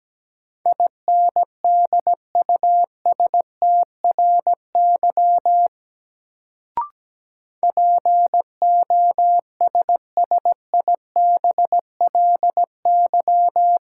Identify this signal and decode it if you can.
Morse code, heard as a keyed tone, about 17 wpm, 700 Hz. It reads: INDUSTRY POSSIBLY
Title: Sentences from Top 1300 Words - New Word Drill Alt - 17wpm